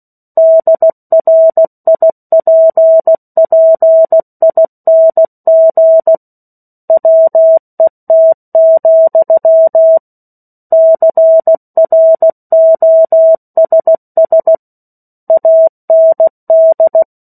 Morse code, heard as a keyed tone, about 16 wpm, 650 Hz. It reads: DRIPPING WET, CROSS AND
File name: 03-Chapter-03 16 WPM